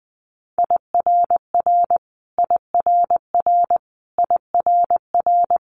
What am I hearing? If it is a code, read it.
Morse code, heard as a keyed tone, about 20 wpm, 700 Hz. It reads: IRR IRR IRR
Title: Multi-letter Phonemes Repeated 3x 20wpm